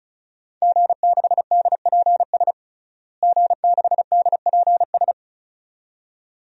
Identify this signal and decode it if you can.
Morse code, heard as a keyed tone, about 36 wpm, 700 Hz. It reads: G6DPS G6DPS
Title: DXCC-100 Call Signs - 5 Characters - Repeated 2x - 35wpm